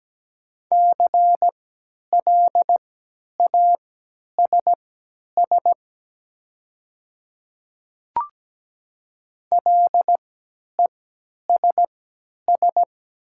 Morse code, heard as a keyed tone, about 17 wpm, spaced as for 10 wpm, 700 Hz. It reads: CLASS LESS
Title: 3x Character Spacing - Top 500 Words - 17wpm